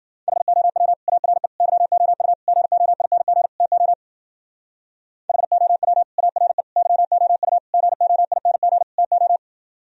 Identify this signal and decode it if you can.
Morse code, heard as a keyed tone, about 48 wpm, 700 Hz. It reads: HOW ARE YOU GOING TO HOW ARE YOU GOING TO
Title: Speed-Racing - Sets of 5 Words 40wpm